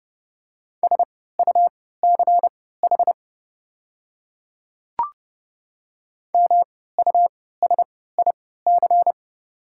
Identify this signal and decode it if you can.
Morse code, heard as a keyed tone, about 30 wpm, spaced as for 17 wpm, 700 Hz. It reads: SUCH MUSIC
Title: 3x Character Spacing - Top 300 Words - 30wpm